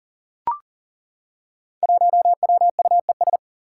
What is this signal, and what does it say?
Morse code, heard as a keyed tone, about 40 wpm, 700 Hz. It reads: 1WUES
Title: Five Challenging Letter-Number Combinations 40wpm